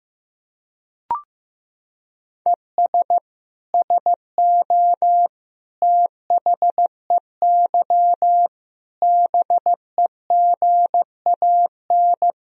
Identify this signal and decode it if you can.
Morse code, heard as a keyed tone, about 15 wpm, 700 Hz. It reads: ES SO THEY BEGAN
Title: Sentences from Top 400 Words and Abbreviations 15wpm